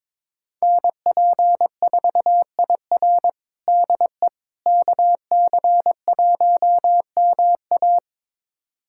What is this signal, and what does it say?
Morse code, heard as a keyed tone, about 22 wpm, 700 Hz. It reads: NP4IR DE KC1MA
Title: USA Callsign DE Callsign - 5 Characters - 22wpm